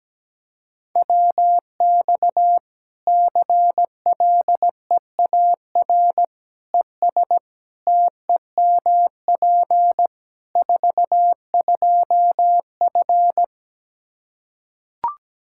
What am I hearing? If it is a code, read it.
Morse code, heard as a keyed tone, about 17 wpm, 700 Hz. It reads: WX CLEAR ES TEMP 42F
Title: Practice QSO - Exchange 2 - Info - 17wpm